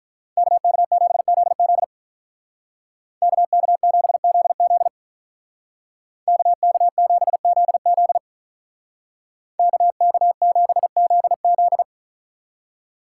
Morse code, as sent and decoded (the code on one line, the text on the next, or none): -.- -.- --... --.. --.. / -.- -.- --... --.. --.. / -.- -.- --... --.. --.. / -.- -.- --... --.. --..
KK7ZZ KK7ZZ KK7ZZ KK7ZZ